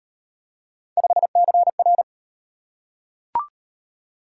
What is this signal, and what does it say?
Morse code, heard as a keyed tone, about 35 wpm, 700 Hz. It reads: 5CR